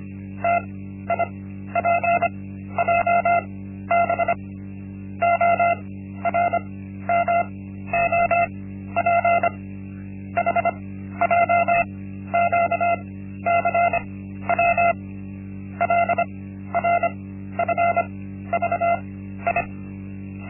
Morse code, heard as a keyed tone, about 25 wpm, spaced as for 13 wpm, 690 Hz. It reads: TIPJB ORMOP HJQCW LRFVI
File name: Linksmakalnis-1